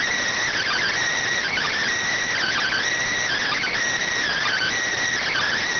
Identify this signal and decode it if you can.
Emergency Alert System (SAME) tones (header bursts), heard as harsh buzzing bursts. It is heard from 0.5 to 4.8 s.
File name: findings / CT AXIAL -572mm 2.gen-38 rx